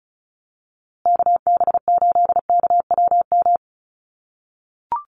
Morse code, heard as a keyed tone, about 34 wpm, 700 Hz. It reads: KB8KWM
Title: USA Call Signs Repeated 3x 35wpm